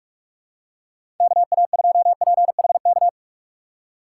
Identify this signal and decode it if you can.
Morse code, heard as a keyed tone, about 46 wpm, 700 Hz. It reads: KA2PHK
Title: USA Call Signs - 6 Characters - 3x Character Spacing - 45wpm